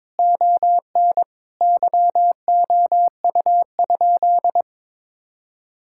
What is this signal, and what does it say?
Morse code, heard as a keyed tone, about 22 wpm, 700 Hz. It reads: ON YOU?
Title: Sentences from Top 1000 Words - Review Part 5 - 22wpm